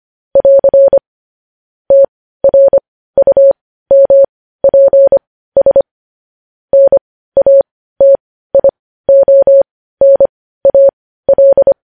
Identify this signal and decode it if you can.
Morse code, heard as a keyed tone, about 25 wpm, spaced as for 15 wpm, 550 Hz. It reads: <AR> TRUMPS NATIONAL